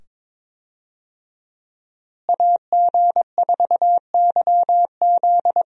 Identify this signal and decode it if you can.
Morse code, heard as a keyed tone, about 22 wpm, 700 Hz. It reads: AG4YZ
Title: USA Call Signs - 5 Characters - Repeated 2x - 22wpm